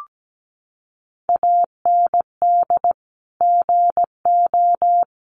Morse code, heard as a keyed tone, about 17 wpm, 700 Hz. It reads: AND GO
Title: Sentences from Top 700 Words 17wpm